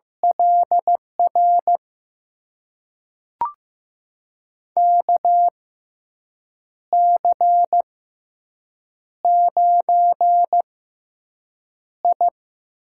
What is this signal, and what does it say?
Morse code, heard as a keyed tone, about 15 wpm, spaced as for 6 wpm, 700 Hz. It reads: LR KC9I
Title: USA Call Signs - 6 Characters - 6x Character Spacing - 15wpm